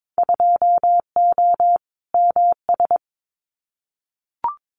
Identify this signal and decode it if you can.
Morse code, heard as a keyed tone, about 22 wpm, 700 Hz. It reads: 2O MS
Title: Winter Field Day Exchanges 22wpm